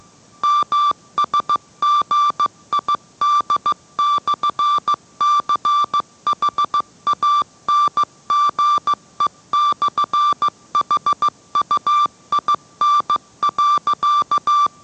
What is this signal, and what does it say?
Morse code, heard as a keyed tone, about 15 wpm, 1170 Hz. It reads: MSGID/CHANGE/HUIN.